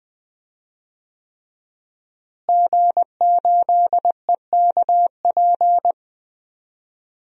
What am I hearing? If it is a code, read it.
Morse code, heard as a keyed tone, about 20 wpm, 700 Hz. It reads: G8EKP
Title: DXCC-100 Call Signs - 5 Characters - Repeated 3x - 20wpm